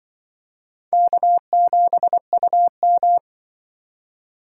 Morse code, heard as a keyed tone, about 25 wpm, 700 Hz. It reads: K7UM